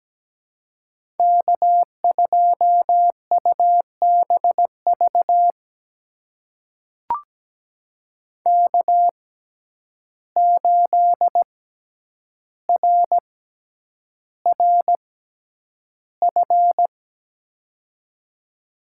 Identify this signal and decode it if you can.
Morse code, heard as a keyed tone, about 17 wpm, spaced as for 9 wpm, 700 Hz. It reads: K2UBV K8RRF
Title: USA Call Signs - 5 Characters - 6x Character Spacing - 17wpm